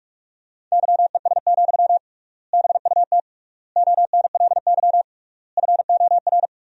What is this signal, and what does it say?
Morse code, heard as a keyed tone, about 48 wpm, 700 Hz. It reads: YES, BUT ONLY FOR
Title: 3x Word-Spacing - Sentences from Top 500 Words - 45wpm